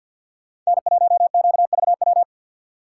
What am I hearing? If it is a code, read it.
Morse code, heard as a keyed tone, about 62 wpm, 700 Hz. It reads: N1QVW
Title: Speed Racing - USA 1x3 Call Signs 50wpm